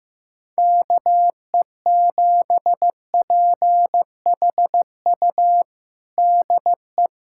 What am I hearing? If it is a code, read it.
Morse code, heard as a keyed tone, about 16 wpm, 700 Hz. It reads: KE7PHU DE
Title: USA Callsign DE Callsign - 6 Characters - 15wpm